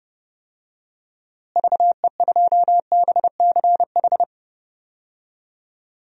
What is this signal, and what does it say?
Morse code, heard as a keyed tone, about 30 wpm, 700 Hz. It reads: VE2BCH
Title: DXCC-100 Call Signs - 6 Characters - Repeated 2x - 30wpm